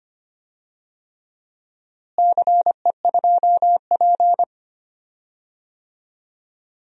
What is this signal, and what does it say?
Morse code, heard as a keyed tone, about 25 wpm, 700 Hz. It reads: CE2P
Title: DXCC-100 Call Signs - 4 Characters - Repeated 3x - 25wpm